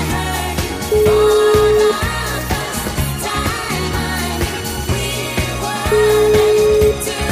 Ringback tone (Europe, 425 Hz), with noise.